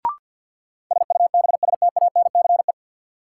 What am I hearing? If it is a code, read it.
Morse code, heard as a keyed tone, about 53 wpm, 700 Hz. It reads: SUBSTANCE
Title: Sentences from Top 1500 Words - New Word Drill - 50wpm